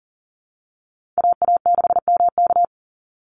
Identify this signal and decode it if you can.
Morse code, heard as a keyed tone, about 40 wpm, 700 Hz. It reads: AA6MK
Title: Call Signs - 40wpm